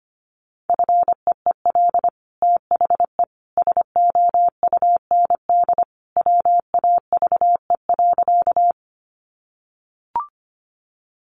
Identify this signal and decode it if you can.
Morse code, heard as a keyed tone, about 25 wpm, 700 Hz. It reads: FEEL THE SOUND WAVE.
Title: Sentences from Top 1400 Words 25wpm